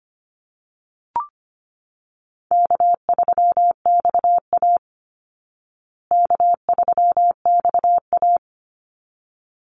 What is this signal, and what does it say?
Morse code, heard as a keyed tone, about 25 wpm, 700 Hz. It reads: K3XA K3XA
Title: Easy Call Signs Repeated 2x - 25wpm